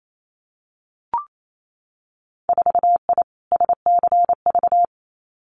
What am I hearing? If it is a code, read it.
Morse code, heard as a keyed tone, about 28 wpm, 690 Hz. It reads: 4I SCV
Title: Winter Field Day Exchanges Encore 28wpm